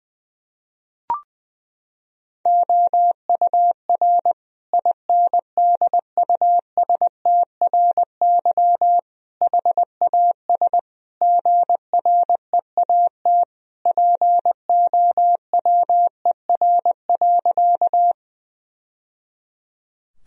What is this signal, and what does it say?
Morse code, heard as a keyed tone, about 20 wpm, 700 Hz. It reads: OUR INDUSTRY HAS GREAT POWER.
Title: Sentences from Top 1000 Words - Review Part 4 - 20wpm